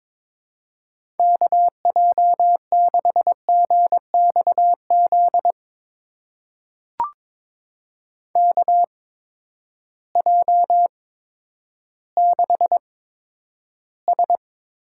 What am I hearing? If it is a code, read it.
Morse code, heard as a keyed tone, about 22 wpm, spaced as for 11 wpm, 700 Hz. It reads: KJ6GXZ KJ6S
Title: USA Call Signs - 6 Characters - 8x Character Spacing - 22wpm